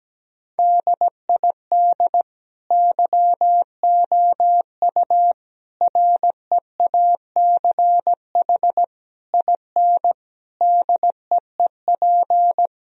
Morse code, heard as a keyed tone, about 17 wpm, 700 Hz. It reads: DID YOU REACH IN DEEP
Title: Sentences from Top 1000 Words - Review Part 3 - 17wpm